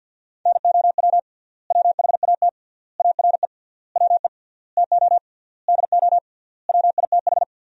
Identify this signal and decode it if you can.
Morse code, heard as a keyed tone, about 50 wpm, 700 Hz. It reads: NOW WHAT ARE WE TO DO WITH
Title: 3x Word-Spacing - Sentences from Top 100 Words - 50wpm